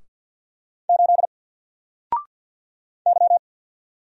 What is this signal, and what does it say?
Morse code, heard as a keyed tone, about 50 wpm, 700 Hz. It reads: 8 Y
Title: Single Letter-Number 50wpm